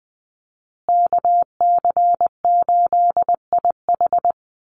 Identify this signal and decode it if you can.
Morse code, heard as a keyed tone, about 20 wpm, 700 Hz. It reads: KC8IH
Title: USA Call Signs Repeated 2x 20wpm